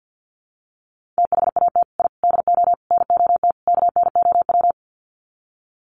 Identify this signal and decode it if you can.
Morse code, heard as a keyed tone, about 50 wpm, 700 Hz. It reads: THAT I DO NOT KNOW
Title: Sets of 5 Words Encore2 50wpm